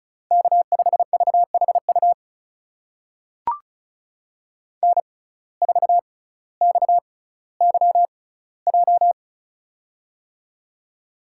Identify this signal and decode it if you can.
Morse code, heard as a keyed tone, about 36 wpm, spaced as for 18 wpm, 700 Hz. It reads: K5VHU N4XYJ
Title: USA Call Signs - 5 Characters - 6x Character Spacing - 35wpm